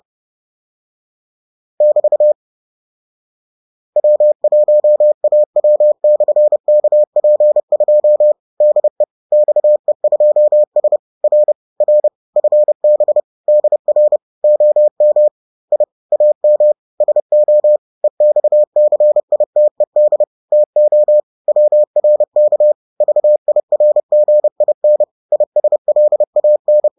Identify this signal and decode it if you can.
Morse code, heard as a keyed tone, about 30 wpm, 600 Hz. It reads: <BT> W1AW/KP2 DE XE2S R R FB DR OM I AM SO EXCITED TO WRK VIRGIN ISLAN
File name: 30wpm-7